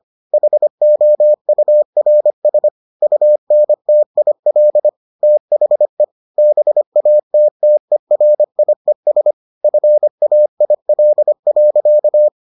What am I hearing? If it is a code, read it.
Morse code, heard as a keyed tone, about 25 wpm, 600 Hz. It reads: HOURS UNTIL THE BATTERIES FAIL.